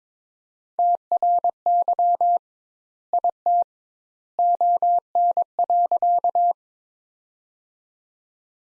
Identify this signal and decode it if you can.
Morse code, heard as a keyed tone, about 22 wpm, 700 Hz. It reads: TRY IT ON.
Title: Sentences Top 200 Words - 22wpm